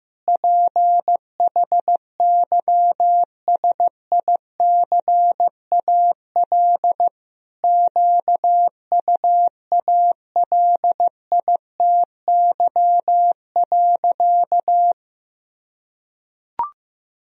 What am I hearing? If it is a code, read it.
Morse code, heard as a keyed tone, about 16 wpm, 700 Hz. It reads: PHYSICAL QUALITY.
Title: Sentences from Top 1000 Words - Review Part 4 - 15wpm